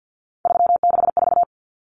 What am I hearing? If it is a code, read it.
Morse code, heard as a keyed tone, about 50 wpm, 700 Hz. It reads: 364
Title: Common Cut Numbers - 3-Digit 50wpm